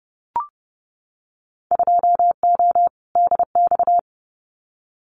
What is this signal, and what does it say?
Morse code, heard as a keyed tone, about 30 wpm, 700 Hz. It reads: 2O DX
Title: Winter Field Day Exchanges Encore 30wpm